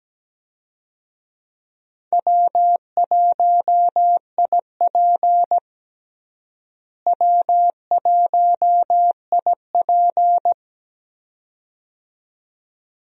Morse code, heard as a keyed tone, about 17 wpm, 700 Hz. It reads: W1IP W1IP
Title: USA Call Signs - 4 Characters - Repeated 2x - 17wpm